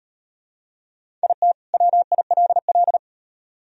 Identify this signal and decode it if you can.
Morse code, heard as a keyed tone, about 33 wpm, 700 Hz. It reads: IT WILL